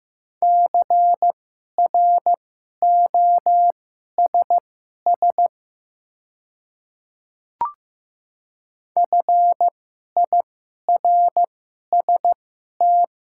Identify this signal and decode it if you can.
Morse code, heard as a keyed tone, about 15 wpm, spaced as for 11 wpm, 690 Hz. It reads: CROSS FIRST
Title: 2x Character Spacing - Top 500 Words - 15wpm